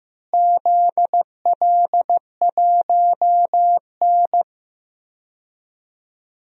Morse code, heard as a keyed tone, about 15 wpm, 700 Hz. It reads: ZL1N
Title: DXCC-100 Call Signs - 4 Characters - Repeated 3x - 15wpm